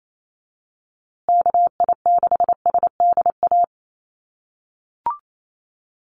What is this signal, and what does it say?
Morse code, heard as a keyed tone, about 28 wpm, 700 Hz. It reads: KI6SDA
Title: USA Call Signs Repeated 3x 28wpm